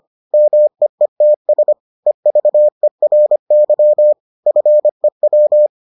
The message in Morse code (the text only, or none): MEETS EVERY FEW